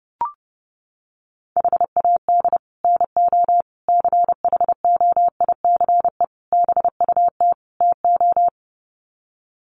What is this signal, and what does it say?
Morse code, heard as a keyed tone, about 30 wpm, 700 Hz. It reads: HAD NO CHOICE BUT TO
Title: Sets of 5 Words 30wpm